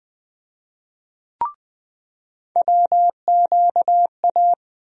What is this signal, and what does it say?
Morse code, heard as a keyed tone, about 20 wpm, 700 Hz. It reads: WQA
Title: USA Call Sign Postfixes - 3 Letters 20wpm